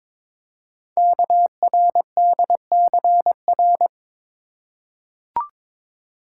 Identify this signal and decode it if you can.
Morse code, heard as a keyed tone, about 22 wpm, 700 Hz. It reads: KRDCR
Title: Code Groups - 5 Letters - 22wpm